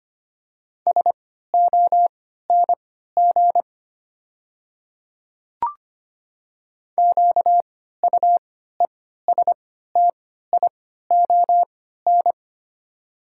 Morse code, heard as a keyed tone, about 25 wpm, spaced as for 14 wpm, 690 Hz. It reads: SONG QUESTION
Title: 3x Character Spacing - Top 500 Words - 25wpm